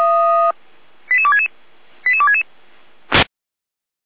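Five-tone selective call (ZVEI), heard as quick runs of five tones, tones of about 70 ms. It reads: 88266 88266
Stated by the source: ZVEI five-tone selective call received by SDR, address 88129, from feuerwehr-orlamuende.org fire brigade server.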